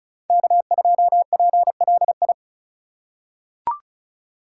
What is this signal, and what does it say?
Morse code, heard as a keyed tone, about 36 wpm, 700 Hz. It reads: K2PLI
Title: USA Call Signs - 5 Characters - 3x Character Spacing - 35wpm